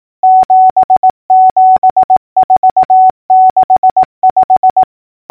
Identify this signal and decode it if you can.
Morse code, heard as a keyed tone, about 18 wpm, 750 Hz. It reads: 77465
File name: numbers / V2-morse-clicks